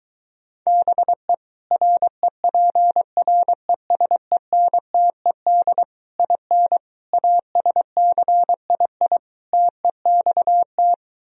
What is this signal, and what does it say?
Morse code, heard as a keyed tone, about 23 wpm, 700 Hz. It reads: BE REPRESENTED IN ASCII TEXT